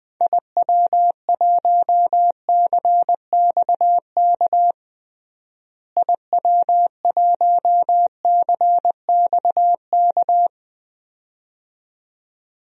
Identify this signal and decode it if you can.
Morse code, heard as a keyed tone, about 20 wpm, 700 Hz. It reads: IW1CXK IW1CXK
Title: DXCC-100 Call Signs - 6 Characters - Repeated 2x - 20wpm